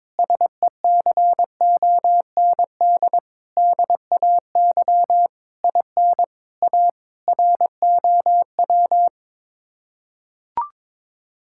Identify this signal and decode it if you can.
Morse code, heard as a keyed tone, about 22 wpm, 700 Hz. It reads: SECOND DAY IN A ROW
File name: practice / 22wpm / Sets of 7 Words - 22wpm